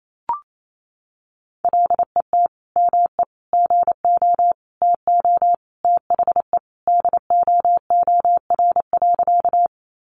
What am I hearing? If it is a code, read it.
Morse code, heard as a keyed tone, about 28 wpm, 700 Hz. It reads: LET ME GO TO THE DOOR.
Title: Sentences from Top 300 Words 28wpm